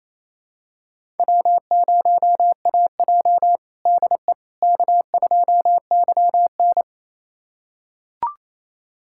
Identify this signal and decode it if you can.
Morse code, heard as a keyed tone, about 28 wpm, 700 Hz. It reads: W0AJ DE K2YN
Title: USA Callsign DE Callsign - 4 Characters - 28wpm